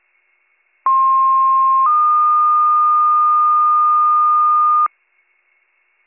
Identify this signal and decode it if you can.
Two-tone page, heard as a one-second tone, then a three-second tone, about 1060 then 1220 Hz. It is heard from 0.9 to 4.9 s.